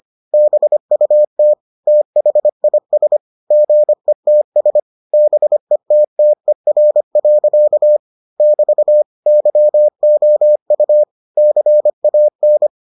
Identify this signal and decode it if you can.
Morse code, heard as a keyed tone, about 25 wpm, 600 Hz. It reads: BUT THIS GETS BETTER. <BT> YOU CAN